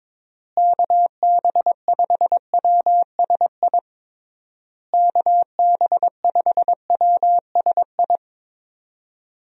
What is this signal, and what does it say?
Morse code, heard as a keyed tone, about 22 wpm, 700 Hz. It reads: KB5WSI KB5WSI
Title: USA Call Signs - 6 Characters - Repeated 3x - 22wpm